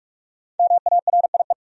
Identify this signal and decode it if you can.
Morse code, heard as a keyed tone, about 46 wpm, 700 Hz. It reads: MARIE
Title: USA Common Names - 45wpm